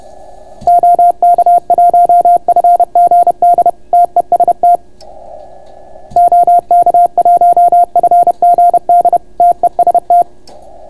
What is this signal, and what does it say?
Morse code, heard as a keyed tone, about 31 wpm, 690 Hz. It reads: OK1FGD TEST OK1FGD TEST